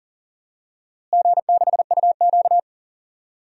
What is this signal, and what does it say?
Morse code, heard as a keyed tone, about 40 wpm, 700 Hz. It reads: G6UQ
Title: DXCC-100 Call Signs - 4 Characters - Repeated 3x - 40wpm